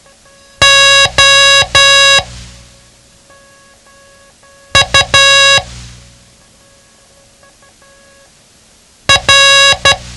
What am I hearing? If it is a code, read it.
Morse code, heard as a keyed tone, about 12 wpm, spaced as for 3 wpm, 1650 Hz. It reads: OUR
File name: ham radio / CodeQuick / Code Quick 07